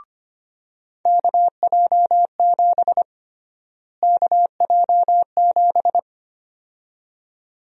Morse code, heard as a keyed tone, about 25 wpm, 700 Hz. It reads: KJ7 KJ7
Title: USA Call Sign Prefixes - Repeated 2x - 25wpm